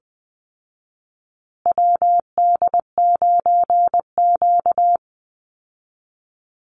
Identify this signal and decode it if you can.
Morse code, heard as a keyed tone, about 20 wpm, 700 Hz. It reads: WD9Q
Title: Call Signs Repeated 3x - 20wpm